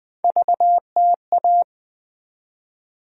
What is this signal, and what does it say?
Morse code, heard as a keyed tone, about 20 wpm, 700 Hz. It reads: VTA
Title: USA Call Sign Postfixes - 3 Letters 20wpm